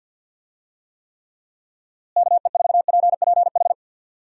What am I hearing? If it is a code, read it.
Morse code, heard as a keyed tone, about 50 wpm, 700 Hz. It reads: KE4PPH
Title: USA Call Signs - 6 Characters - Repeated 3x - 50wpm